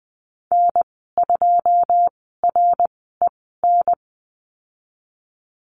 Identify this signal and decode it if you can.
Morse code, heard as a keyed tone, about 20 wpm, spaced as for 15 wpm, 700 Hz. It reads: N2REN
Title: 088 Lesson 15g - Number 5 - Call Signs 20wpm